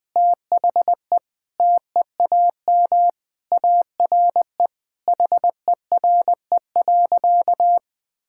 Morse code, heard as a keyed tone, about 20 wpm, 700 Hz. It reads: THE TEAM ARE HERE.
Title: Sentences with 3-Word Phrases - 20wpm